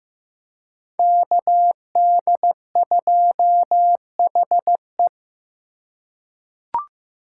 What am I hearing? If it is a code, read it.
Morse code, heard as a keyed tone, about 15 wpm, 700 Hz. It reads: KD2HE
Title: K1USN SST QSO Practice - 15wpm